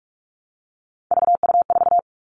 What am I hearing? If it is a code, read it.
Morse code, heard as a keyed tone, about 44 wpm, 700 Hz. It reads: VU4
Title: All Cut Numbers - 3-Digit 45wpm